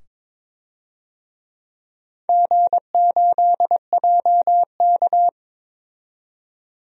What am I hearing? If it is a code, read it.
Morse code, heard as a keyed tone, about 21 wpm, 700 Hz. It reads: G8JK
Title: DXCC-100 Call Signs - 4 Characters - 22wpm